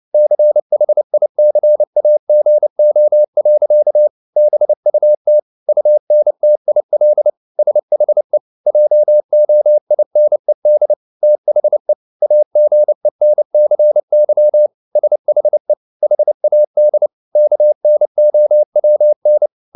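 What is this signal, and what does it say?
Morse code, heard as a keyed tone, about 29 wpm, 600 Hz. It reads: CHICAGO. BUT UNTIL SHE JOINED THE AGENCY SHE HAD KNOWN
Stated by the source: Morse code at 29 wpm, sending Ken Follett's novel Never in English, from CW med Gnister.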